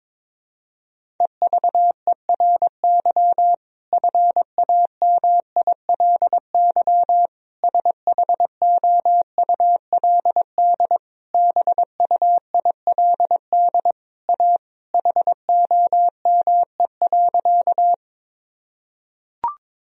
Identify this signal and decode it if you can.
Morse code, heard as a keyed tone, about 22 wpm, 700 Hz. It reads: EVERY FAMILY SHOULD BUILD A HOME.
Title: Sentences from Top 1000 Words - Review Part 1 - 22wpm